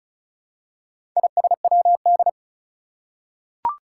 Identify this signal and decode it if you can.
Morse code, heard as a keyed tone, about 36 wpm, 700 Hz. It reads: ISWD
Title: Four Challenging Letter-Number Combinations 35wpm